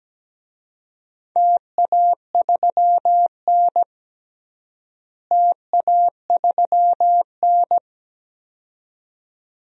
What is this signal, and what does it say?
Morse code, heard as a keyed tone, about 17 wpm, 700 Hz. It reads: TA3N TA3N
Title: DXCC-100 Call Signs - 4 Characters - Repeated 2x - 17wpm